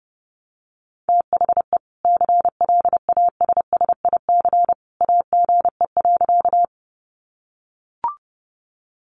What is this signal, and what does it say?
Morse code, heard as a keyed tone, about 30 wpm, 700 Hz. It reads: THE CLASSIC AGE.
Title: Sentences from Top 1000 Words 30wpm